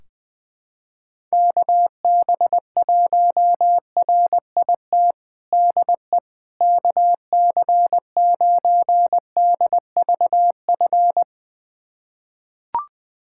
Morse code, heard as a keyed tone, about 20 wpm, 700 Hz. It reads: KB1RIT DE KC9DVF
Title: USA Callsign DE Callsign - 6 Characters - 20wpm